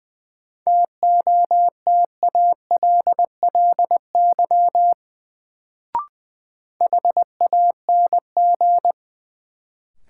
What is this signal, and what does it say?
Morse code, heard as a keyed tone, about 20 wpm, 700 Hz. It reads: TOTALLY HANG
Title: Sentences from Top 1600 Words - New Word Drill Alt - 20wpm